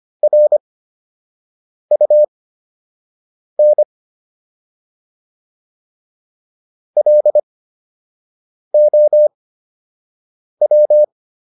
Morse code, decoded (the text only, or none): RUN LOW